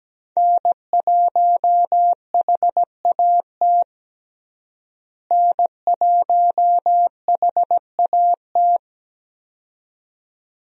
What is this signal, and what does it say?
Morse code, heard as a keyed tone, about 17 wpm, 700 Hz. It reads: N1HAT N1HAT
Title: USA Call Signs - 5 Characters - Repeated 2x - 17wpm